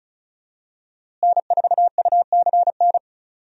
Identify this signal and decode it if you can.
Morse code, heard as a keyed tone, about 34 wpm, 700 Hz. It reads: N4UCN
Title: USA Call Signs - 5 Characters - Repeated 3x - 35wpm